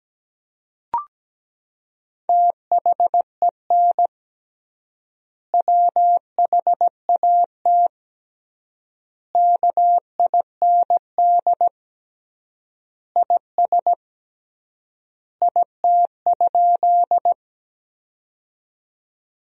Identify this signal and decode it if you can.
Morse code, heard as a keyed tone, about 17 wpm, 700 Hz. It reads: THEN WHAT KIND IS IT?
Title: 3x Word-Spacing - Sentences from Top 200 Words - 17wpm